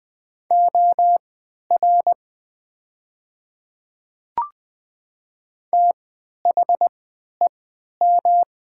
Morse code, heard as a keyed tone, about 20 wpm, spaced as for 11 wpm, 700 Hz. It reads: OR THEM
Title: 3x Character Spacing - Top 200 Words - 20wpm